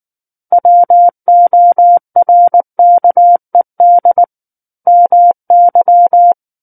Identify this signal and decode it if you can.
Morse code, heard as a keyed tone, about 19 wpm, 700 Hz. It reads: WORKED MY
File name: Book One / 10-Chapter 10 - 19 WPM